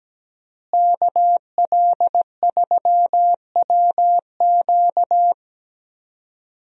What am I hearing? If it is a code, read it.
Morse code, heard as a keyed tone, about 17 wpm, 700 Hz. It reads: KL3WQ
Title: USA Call Signs - 5 Characters - Repeated 3x - 17wpm